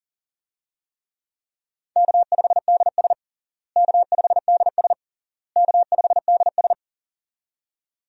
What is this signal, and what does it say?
Morse code, heard as a keyed tone, about 40 wpm, 700 Hz. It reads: K5DS K5DS K5DS
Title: USA Call Signs - 4 Characters - Repeated 3x - 40wpm